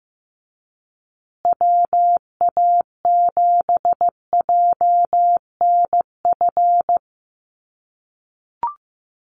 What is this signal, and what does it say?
Morse code, heard as a keyed tone, about 15 wpm, 700 Hz. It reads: WA7JNF
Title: USA Call Signs Repeated 3x 15wpm